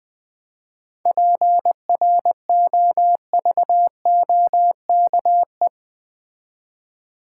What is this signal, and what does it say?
Morse code, heard as a keyed tone, about 20 wpm, 700 Hz. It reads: PROVOKE
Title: 7 Letter Words - First 3 Letters Given - 20wpm